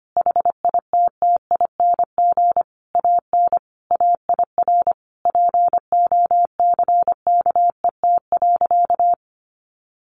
Morse code, heard as a keyed tone, about 25 wpm, 700 Hz. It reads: HITTING AN AIR POCKET.
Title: A Sound Of Thunder 25wpm